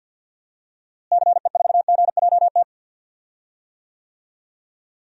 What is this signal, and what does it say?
Morse code, heard as a keyed tone, about 51 wpm, 700 Hz. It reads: CE4GJT
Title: DXCC-100 Call Signs - 6 Characters - Repeated 3x - 50wpm